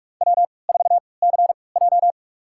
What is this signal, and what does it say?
Morse code, heard as a keyed tone, about 46 wpm, spaced as for 26 wpm, 700 Hz. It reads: W4CJ
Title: USA Call Signs - 3x Character Spacing - 45wpm